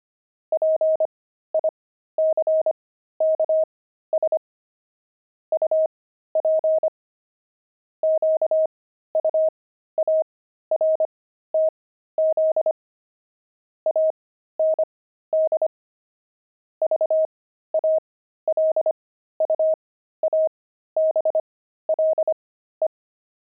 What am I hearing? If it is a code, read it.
Morse code, heard as a keyed tone, about 25 wpm, spaced as for 13 wpm, 630 Hz. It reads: PICKS UP QUARTZ AND VALUABLE
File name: pangram-fs-13-cs-25-grandpa